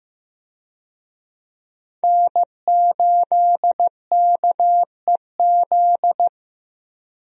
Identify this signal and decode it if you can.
Morse code, heard as a keyed tone, about 15 wpm, 700 Hz. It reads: N8KEZ
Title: USA Call Signs - 5 Characters - Repeated 2x - 15wpm